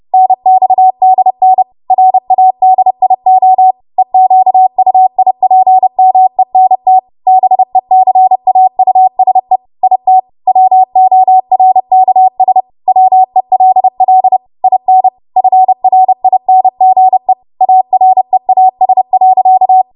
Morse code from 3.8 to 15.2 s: EQUIPMENT BECAUSE IT WORKS WELL IN